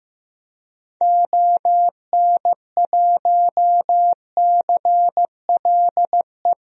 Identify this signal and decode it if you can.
Morse code, heard as a keyed tone, about 16 wpm, 700 Hz. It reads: ON1CLE
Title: DXCC-100 Call Signs - 6 Characters - Repeated 2x - 15wpm